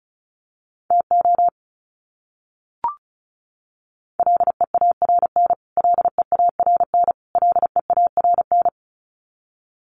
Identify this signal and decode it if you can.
Morse code, heard as a keyed tone, about 34 wpm, 700 Hz. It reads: TO LEARN LEARN LEARN
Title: Top 500 Words Repeated 35wpm